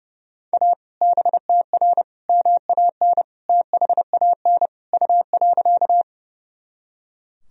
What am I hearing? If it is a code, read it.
Morse code, heard as a keyed tone, about 30 wpm, 700 Hz. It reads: A BTR MAN THAN U.
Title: Sentences from Top 400 Words and Abbreviations 30wpm